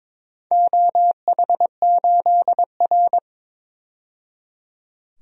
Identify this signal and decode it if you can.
Morse code, heard as a keyed tone, about 22 wpm, 700 Hz. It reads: OH8R
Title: DXCC-100 Call Signs - 4 Characters - 22wpm